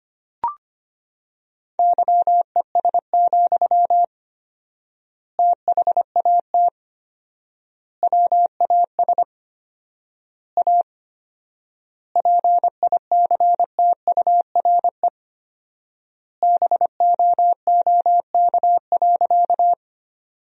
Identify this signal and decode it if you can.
Morse code, heard as a keyed tone, about 25 wpm, 700 Hz. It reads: YES, THAT WAS A PICTURE BOOK.
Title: 4x Word-Spacing - Sentences from Top 500 Words - 25wpm